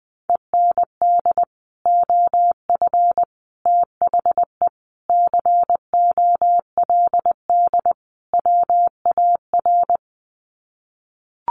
Morse code, heard as a keyed tone, about 20 wpm, 700 Hz. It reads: END OF THE COLD WAR
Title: Sets of 5 Words Encore 20wpm